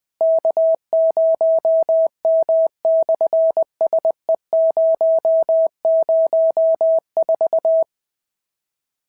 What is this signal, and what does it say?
Morse code, heard as a keyed tone, about 20 wpm, 650 Hz. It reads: K0M/SE004
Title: SOTA QSO Practice 20wpm